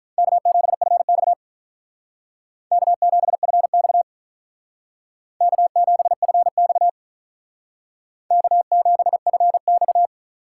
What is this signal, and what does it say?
Morse code, heard as a keyed tone, about 46 wpm, 700 Hz. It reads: K7FX K7FX K7FX K7FX